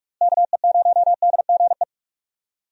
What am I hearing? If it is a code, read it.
Morse code, heard as a keyed tone, about 42 wpm, 700 Hz. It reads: KE0DGE
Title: USA Call Signs - 6 Characters - Repeated 2x - 45wpm